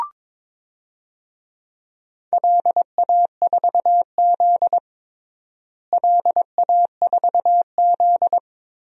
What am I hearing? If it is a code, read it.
Morse code, heard as a keyed tone, about 22 wpm, 700 Hz. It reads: LA4Z LA4Z
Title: DXCC-100 Call Signs - 4 Characters - Repeated 2x - 22wpm